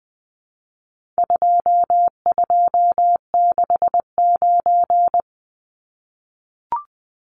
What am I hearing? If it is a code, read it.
Morse code, heard as a keyed tone, about 20 wpm, 700 Hz. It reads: 2269
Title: Numbers - 4-Digit 20wpm